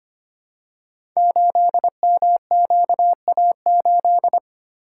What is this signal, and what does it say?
Morse code, heard as a keyed tone, about 25 wpm, 700 Hz. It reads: 8MQA8
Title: Code Groups - 5 Letter:Numbers - 25wpm